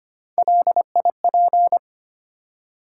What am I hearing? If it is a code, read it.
Morse code, heard as a keyed tone, about 25 wpm, 700 Hz. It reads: LIP
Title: Three Letter Words - 25wpm